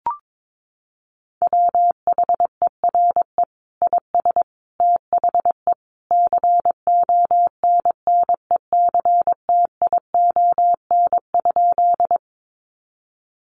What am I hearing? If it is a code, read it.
Morse code, heard as a keyed tone, about 22 wpm, 700 Hz. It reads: WHERE IS THE CONNECTION?
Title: Sentences from Top 1300 Words 22wpm